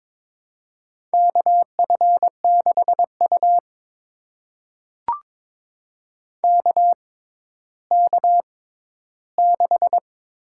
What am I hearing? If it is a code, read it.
Morse code, heard as a keyed tone, about 22 wpm, spaced as for 13 wpm, 700 Hz. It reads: KF6U KK6
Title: USA Call Signs - 6x Character Spacing - 22wpm